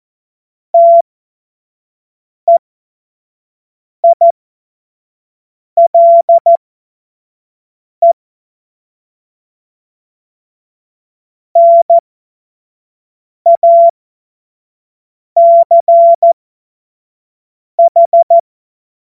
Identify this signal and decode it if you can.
Morse code, heard as a keyed tone, about 14 wpm, spaced as for 5 wpm, 680 Hz. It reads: TEILE NACH
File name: Morsekurs / Witze / Witz 03 Juni 2025 25 F70